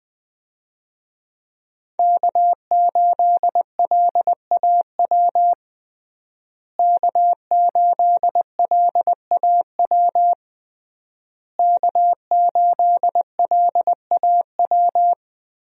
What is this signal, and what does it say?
Morse code, heard as a keyed tone, about 20 wpm, 700 Hz. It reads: K8LAW K8LAW K8LAW
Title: USA Call Signs - 5 Characters - Repeated 3x - 20wpm